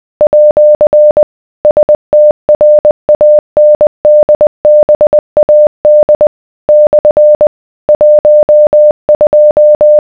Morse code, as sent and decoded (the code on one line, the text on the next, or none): .--.-. / ... - .-. .- -. -.. -... .- -.. / -..-. / .---- ..---
@ STRANDBAD / 12